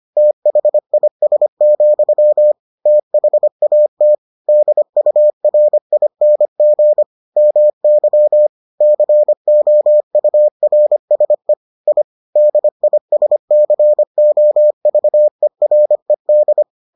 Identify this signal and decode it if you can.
Morse code, heard as a keyed tone, about 25 wpm, 600 Hz. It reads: THIS, THAT DURING MY COURSE I DISCOVERED